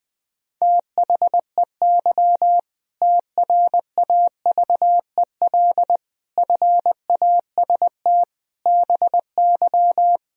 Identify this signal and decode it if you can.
Morse code, heard as a keyed tone, about 20 wpm, 700 Hz. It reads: THEY TRAVEL FAST BY
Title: Sentences from Top 1000 Words - Review Part 3 - 20wpm